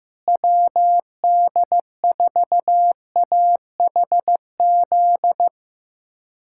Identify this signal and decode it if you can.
Morse code, heard as a keyed tone, about 15 wpm, 700 Hz. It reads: WD4AHZ
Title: K1USN SST QSO Practice - 15wpm